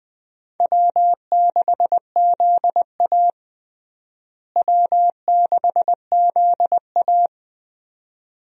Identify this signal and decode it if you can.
Morse code, heard as a keyed tone, about 20 wpm, 700 Hz. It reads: W6ZA W6ZA
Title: USA Call Signs - 4 Characters - Repeated 2x - 20wpm